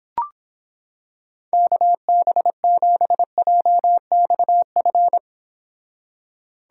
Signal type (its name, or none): Morse code